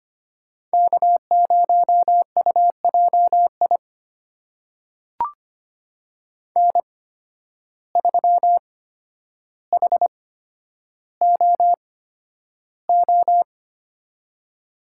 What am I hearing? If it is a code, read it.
Morse code, heard as a keyed tone, about 25 wpm, spaced as for 11 wpm, 700 Hz. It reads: K0UJI N3HOO
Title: USA Call Signs - 5 Characters - 8x Character Spacing - 25wpm